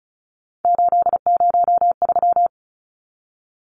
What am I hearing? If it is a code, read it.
Morse code, heard as a keyed tone, about 34 wpm, 700 Hz. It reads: 803
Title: Numbers - 3-Digit 35wpm